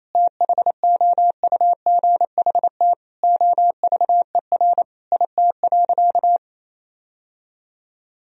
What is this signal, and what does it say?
Morse code, heard as a keyed tone, about 28 wpm, 700 Hz. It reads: THOUGHT OVER IT.
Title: Sentences from Top 1000 Words - Review Part 3 - 28wpm